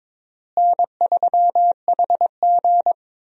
Morse code, heard as a keyed tone, about 22 wpm, 700 Hz. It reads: N3HG
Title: USA Call Signs - 4 Characters - Repeated 3x - 22wpm